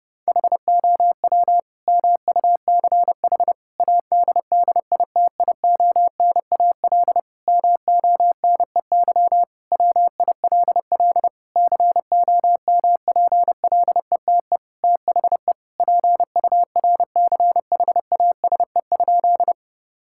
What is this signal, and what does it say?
Morse code, heard as a keyed tone, about 30 wpm, 700 Hz. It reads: HOW MUCH ADDITIONAL MONEY WILL COMPLETE THE PURCHASE?
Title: Sentences from Top 1000 Words - Review Part 5 - 30wpm